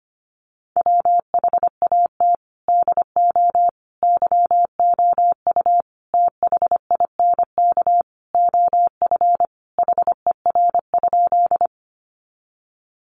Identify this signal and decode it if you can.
Morse code, heard as a keyed tone, about 25 wpm, 700 Hz. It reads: WHAT DO YOU THINK OF HER?
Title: Sentences from Top 200 Words 25wpm